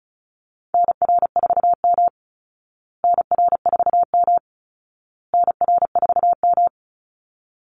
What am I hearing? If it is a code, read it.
Morse code, heard as a keyed tone, about 34 wpm, 700 Hz. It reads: NR4M NR4M NR4M
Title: Call Signs Repeated 2x - 35wpm